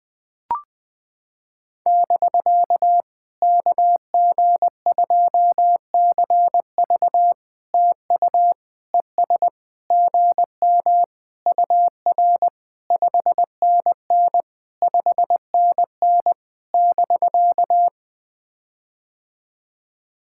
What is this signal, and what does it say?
Morse code, heard as a keyed tone, about 20 wpm, 700 Hz. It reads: <BK> KG2CV TU ES GM UR 5NN 5NN <BK>
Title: POTA QSO Practice - 20wpm